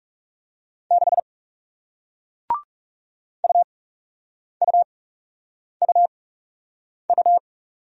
Morse code, heard as a keyed tone, about 39 wpm, spaced as for 8 wpm, 700 Hz. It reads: 6 UUUU